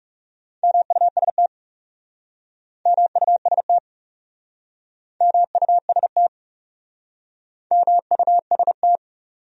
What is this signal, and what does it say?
Morse code, heard as a keyed tone, about 37 wpm, 700 Hz. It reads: MUST MUST MUST MUST
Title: Speed-Racing Top 300 Words 30wpm